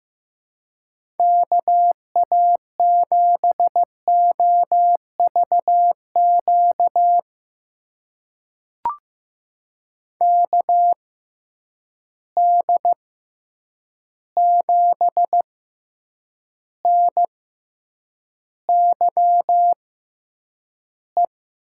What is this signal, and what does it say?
Morse code, heard as a keyed tone, about 15 wpm, spaced as for 8 wpm, 700 Hz. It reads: KA7OVQ KD7NYE
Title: USA Call Signs - 6 Characters - 6x Character Spacing - 15wpm